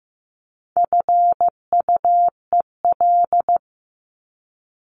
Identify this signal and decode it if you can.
Morse code, heard as a keyed tone, about 15 wpm, 700 Hz. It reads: FUEL
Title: Top 1751-2000 Words 15wpm